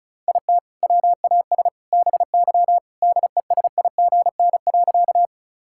Morse code, heard as a keyed tone, about 36 wpm, 700 Hz. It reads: IT WAS BY DESIGN.
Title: Sentences from Top 1000 Words - Review Part 5 - 35wpm